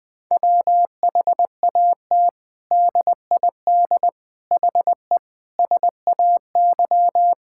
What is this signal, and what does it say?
Morse code, heard as a keyed tone, about 20 wpm, 700 Hz. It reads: WHAT DID HE SAY
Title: Sentences from Top 1000 Words - Review Part 6 - 20wpm